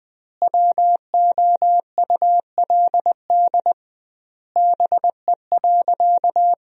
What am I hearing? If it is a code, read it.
Morse code, heard as a keyed tone, about 21 wpm, 700 Hz. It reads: WOULD BE.
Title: 2x Word-Spacing - Sentences from Top 200 Words - 20wpm